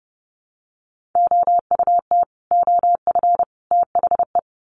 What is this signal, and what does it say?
Morse code, heard as a keyed tone, about 30 wpm, 700 Hz. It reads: OUT OF THE
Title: Sets of 3 Words 30wpm